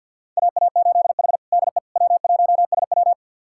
Morse code, heard as a keyed tone, about 46 wpm, 700 Hz. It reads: AA8H DE W1SW
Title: USA Callsign DE Callsign - 4 Characters - 50wpm